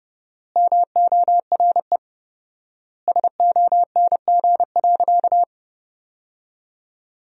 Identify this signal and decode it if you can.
Morse code, heard as a keyed tone, about 32 wpm, 700 Hz. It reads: MORE SONG.
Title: 4x Word-Spacing - Sentences from Top 400 Words - 30wpm